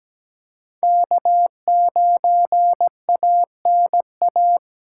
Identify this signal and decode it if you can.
Morse code, heard as a keyed tone, about 17 wpm, 700 Hz. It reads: K9ANA
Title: USA Call Signs - 5 Characters - 8x Character Spacing - 17wpm